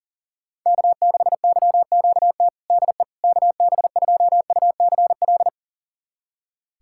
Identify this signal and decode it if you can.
Morse code, heard as a keyed tone, about 42 wpm, 700 Hz. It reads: K6YQT DE KB2UCL
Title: USA Callsign DE Callsign 40wpm